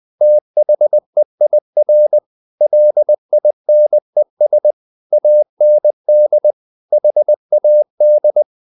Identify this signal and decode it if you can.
Morse code, heard as a keyed tone, about 20 wpm, 600 Hz. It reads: THEIR LINES AND HAD
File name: the old man and the sea / the old man and the sea part 1 20wpm